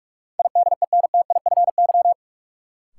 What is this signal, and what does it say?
Morse code, heard as a keyed tone, about 44 wpm, 700 Hz. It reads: IDENTIFY
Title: Sentences from Top 900 Words - New Word Drill - 45wpm